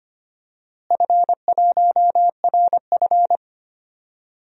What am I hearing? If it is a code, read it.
Morse code, heard as a keyed tone, about 25 wpm, 700 Hz. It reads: F1RF
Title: DXCC-100 Call Signs - 4 Characters - Repeated 3x - 25wpm